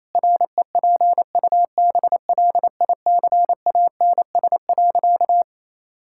Morse code, heard as a keyed tone, about 28 wpm, 700 Hz. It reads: REPUBLICANS.
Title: Pangram Challenge v2 - 28wpm